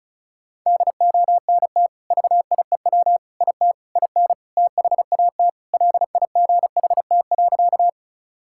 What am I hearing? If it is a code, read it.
Morse code, heard as a keyed tone, about 34 wpm, 700 Hz. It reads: DONT VIEW IT IN THAT LIGHT.
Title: Sentences from Top 1000 Words - Review Part 6 - 35wpm